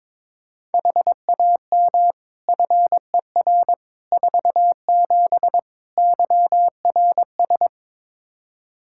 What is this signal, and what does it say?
Morse code, heard as a keyed tone, about 22 wpm, 700 Hz. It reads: HAM FER 47 YRS
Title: K1USN SST QSO Practice 22wpm